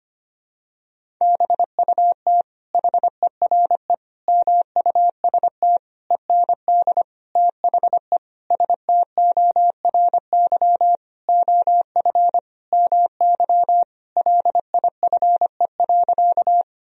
Morse code, heard as a keyed tone, about 25 wpm, 700 Hz. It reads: BUT HERE MUST END THE STORY OF MY LIFE.
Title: Sentences from Top 1000 Words - Review Part 1 - 25wpm